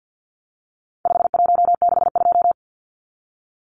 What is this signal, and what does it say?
Morse code, heard as a keyed tone, about 50 wpm, 700 Hz. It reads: 5162
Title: Numbers - 4-Digit 50wpm